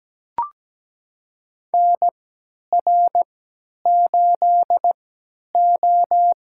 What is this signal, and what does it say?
Morse code, heard as a keyed tone, about 17 wpm, spaced as for 10 wpm, 700 Hz. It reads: NR8O
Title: USA Call Signs - 4 Characters - 3x Character Spacing - 17wpm